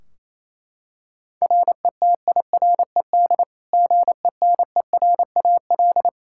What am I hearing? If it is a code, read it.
Morse code, heard as a keyed tone, about 27 wpm, 700 Hz. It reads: RETIRED GENERAL
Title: Practice QSO - Exchange 3 - More Info - 28wpm